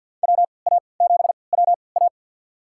Morse code, heard as a keyed tone, about 46 wpm, spaced as for 27 wpm, 700 Hz. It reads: WA7WA
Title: USA Call Signs - 5 Characters - 3x Character Spacing - 50wpm